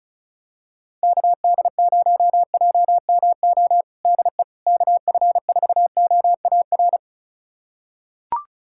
Morse code, heard as a keyed tone, about 36 wpm, 700 Hz. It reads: KD0JMO DE KF4OAR
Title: USA Callsign DE Callsign - 6 Characters - 35wpm